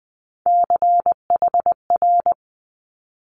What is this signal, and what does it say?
Morse code, heard as a keyed tone, about 20 wpm, 700 Hz. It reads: CHR
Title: Multi-letter Phonemes Quick Fire 20wpm